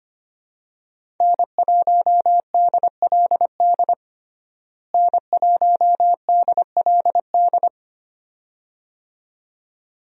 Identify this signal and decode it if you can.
Morse code, heard as a keyed tone, about 25 wpm, 700 Hz. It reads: N1DLD N1DLD
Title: USA Call Signs - 5 Characters - Repeated 2x - 25wpm